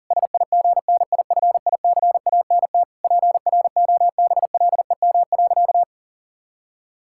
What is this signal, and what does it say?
Morse code, heard as a keyed tone, about 37 wpm, 700 Hz. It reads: SIGNIFICANT PROBLEM.
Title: Sentences from Top 1000 Words - Review Part 6 - 40wpm